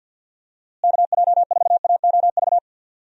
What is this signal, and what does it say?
Morse code, heard as a keyed tone, about 47 wpm, 700 Hz. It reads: KJ4AOV
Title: USA Call Signs - 6 Characters - Repeated 2x - 50wpm